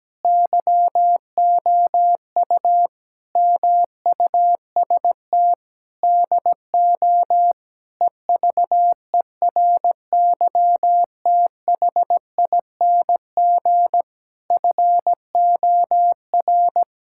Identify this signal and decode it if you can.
Morse code, heard as a keyed tone, about 17 wpm, 700 Hz. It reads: YOU MUST DO EVERYTHING FOR
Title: Sentences from Top 1000 Words - Review Part 6 - 17wpm